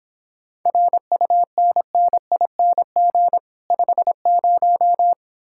Morse code, heard as a keyed tone, about 29 wpm, 700 Hz. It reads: RUNNING 50